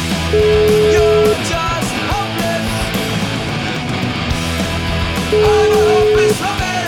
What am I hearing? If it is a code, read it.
Ringback tone (Europe, 425 Hz), with noise.